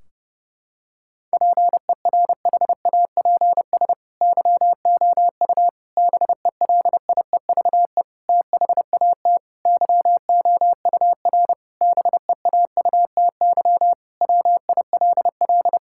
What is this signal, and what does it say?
Morse code, heard as a keyed tone, about 30 wpm, 700 Hz. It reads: PERHAPS YOU BELIEVE THAT YOUR BEAUTY WILL
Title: Sentences from Top 1600 Words 30wpm